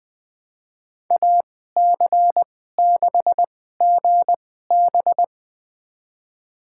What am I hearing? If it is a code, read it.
Morse code, heard as a keyed tone, about 21 wpm, spaced as for 15 wpm, 700 Hz. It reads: AC6GB
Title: USA Call Signs - 2x Character Spacing - 20wpm